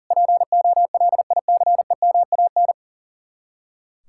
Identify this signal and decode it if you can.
Morse code, heard as a keyed tone, about 52 wpm, 700 Hz. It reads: POLICEMAN